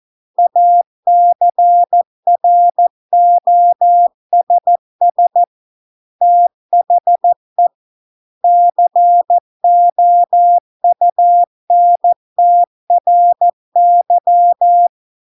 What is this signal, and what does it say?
Morse code, heard as a keyed tone, about 14 wpm, 700 Hz. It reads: ACROSS THE COUNTRY